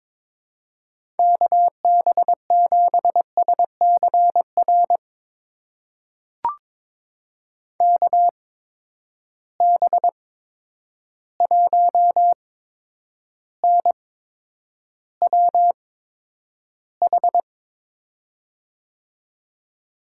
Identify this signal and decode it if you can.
Morse code, heard as a keyed tone, about 23 wpm, spaced as for 10 wpm, 700 Hz. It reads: KB7SCR KB1NWH
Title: USA Call Signs - 6 Characters - 8x Character Spacing - 22wpm